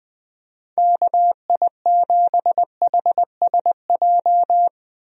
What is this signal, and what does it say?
Morse code, heard as a keyed tone, about 21 wpm, 700 Hz. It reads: KI7HSJ
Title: USA Call Signs - 6 Characters - 20wpm